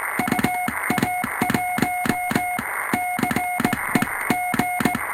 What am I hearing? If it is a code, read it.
Morse code, heard as a keyed tone, about 18 wpm, 720 Hz, with noise. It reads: UA1CEG
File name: attachments / 20.07.2019 manipuljacia home made